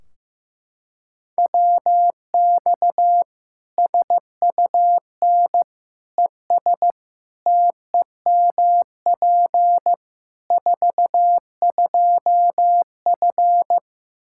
Morse code, heard as a keyed tone, about 15 wpm, 700 Hz. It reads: WX SUN ES TEMP 42F
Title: K1USN SST QSO Practice 15wpm